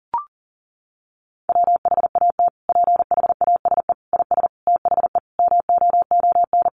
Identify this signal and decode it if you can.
Morse code, heard as a keyed tone, about 40 wpm, 700 Hz. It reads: WHAT PHASE IS THE MOON
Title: Sentences from Top 1200 Words 40wpm